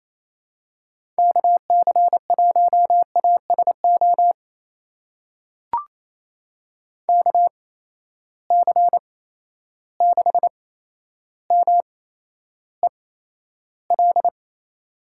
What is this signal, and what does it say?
Morse code, heard as a keyed tone, about 27 wpm, spaced as for 12 wpm, 700 Hz. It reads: KC1ASO KC6MEL
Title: USA Call Signs - 6 Characters - 8x Character Spacing - 28wpm